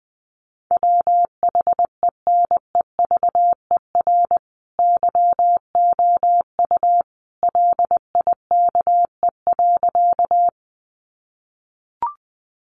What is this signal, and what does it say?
Morse code, heard as a keyed tone, about 20 wpm, 700 Hz. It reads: WHENEVER YOU LIKE.
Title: Sentences from Top 1200 Words 20wpm